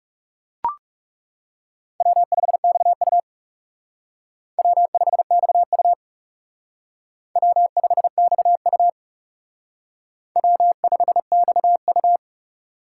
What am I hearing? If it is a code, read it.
Morse code, heard as a keyed tone, about 38 wpm, 700 Hz. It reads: W5XU W5XU W5XU W5XU
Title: Speed Racing - USA 1x2 Call Signs 30wpm